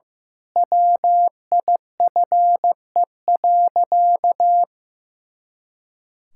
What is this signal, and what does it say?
Morse code, heard as a keyed tone, about 15 wpm, 700 Hz. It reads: WIFE.
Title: Pangram Challenge v2 - 15wpm